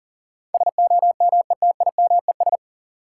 Morse code, heard as a keyed tone, about 39 wpm, 700 Hz. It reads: SOMETIMES
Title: Sentences from Top 600 Words - New Word Drill - 40wpm